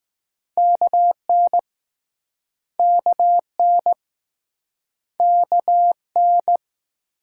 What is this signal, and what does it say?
Morse code, heard as a keyed tone, about 21 wpm, 700 Hz. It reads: KN KN K N